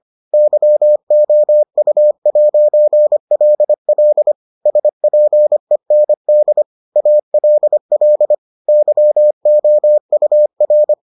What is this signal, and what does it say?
Morse code, heard as a keyed tone, about 25 wpm, 600 Hz. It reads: YOU'LL SPEND ALL YOUR